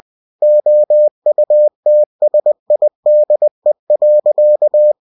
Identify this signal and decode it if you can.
Morse code, heard as a keyed tone, about 20 wpm, 600 Hz. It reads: OUTSIDE.
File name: cw-books / winnie-the-pooh / winnie-20wpm-page-0031